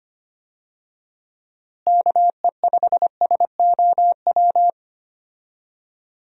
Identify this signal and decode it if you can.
Morse code, heard as a keyed tone, about 25 wpm, 700 Hz. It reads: KE5SOW
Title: USA Call Signs - 6 Characters - Repeated 3x - 25wpm